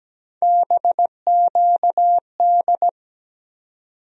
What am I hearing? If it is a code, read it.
Morse code, heard as a keyed tone, about 17 wpm, 700 Hz. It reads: BQD
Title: USA Call Sign Postfixes - 3 Letters 17wpm